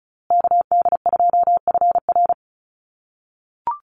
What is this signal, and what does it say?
Morse code, heard as a keyed tone, about 34 wpm, 700 Hz. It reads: KD2FR
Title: USA Call Signs Repeated 2x 35wpm